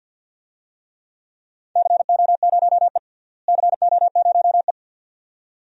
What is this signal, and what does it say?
Morse code, heard as a keyed tone, about 48 wpm, 700 Hz. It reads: CO0E CO0E
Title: DXCC-100 Call Signs - 4 Characters - Repeated 2x - 50wpm